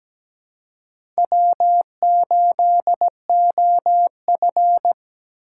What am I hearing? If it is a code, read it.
Morse code, heard as a keyed tone, about 17 wpm, 700 Hz. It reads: W8OF
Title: USA Call Signs - 2x Character Spacing - 17wpm